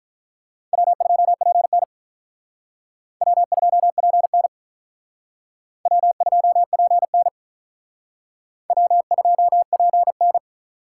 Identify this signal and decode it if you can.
Morse code, heard as a keyed tone, about 46 wpm, 700 Hz. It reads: W2PN W2PN W2PN W2PN